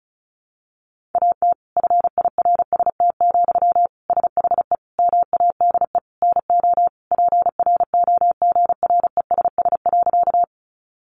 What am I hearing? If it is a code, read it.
Morse code, heard as a keyed tone, about 34 wpm, 700 Hz. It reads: AT FIRST, SHE MADE NO PROGRESS.
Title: Sentences from Top 1400 Words 35wpm